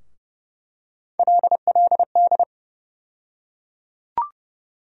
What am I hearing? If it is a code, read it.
Morse code, heard as a keyed tone, about 30 wpm, 700 Hz. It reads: LLD
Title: USA Call Sign Postfixes - 3 Letters 30wpm